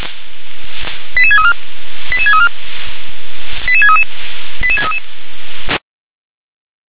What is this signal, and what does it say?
Five-tone selective call (ZVEI), heard as quick runs of five tones, tones of about 70 ms. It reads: 88634 88634 88633 88633
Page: SDR/ZVEI